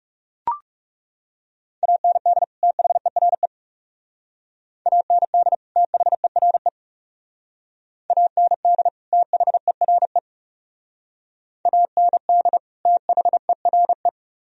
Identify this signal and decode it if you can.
Morse code, heard as a keyed tone, about 37 wpm, 700 Hz. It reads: AND THERE AND THERE AND THERE AND THERE
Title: Speed-Racing - Sets of 2 Words 30wpm